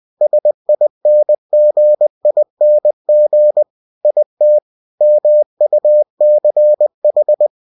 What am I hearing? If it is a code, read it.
Morse code, heard as a keyed tone, about 20 wpm, 600 Hz. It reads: SINGING IT MUCH
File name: cw-books / winnie-the-pooh / winnie-20wpm-page-0076